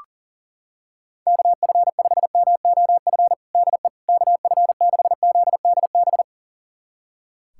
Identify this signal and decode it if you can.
Morse code, heard as a keyed tone, about 40 wpm, 700 Hz. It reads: KF5MOF DE KF6ZDB
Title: USA Callsign DE Callsign - 6 Characters - 40wpm